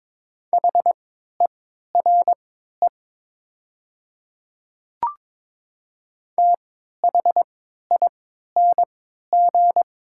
Morse code, heard as a keyed tone, about 22 wpm, spaced as for 12 wpm, 700 Hz. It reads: HERE THING
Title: 3x Character Spacing - Top 300 Words - 22wpm